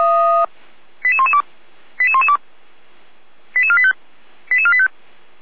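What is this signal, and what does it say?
Five-tone selective call (ZVEI), heard as quick runs of five tones, tones of about 70 ms. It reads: 88112 88112 88485 88485
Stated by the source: ZVEI five-tone selective call received on an SDR.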